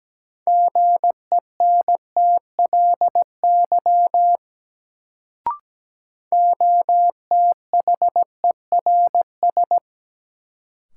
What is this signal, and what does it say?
Morse code, heard as a keyed tone, about 17 wpm, 700 Hz. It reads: GENTLY OTHERS
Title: Sentences from Top 1500 Words - New Word Drill Alt - 17wpm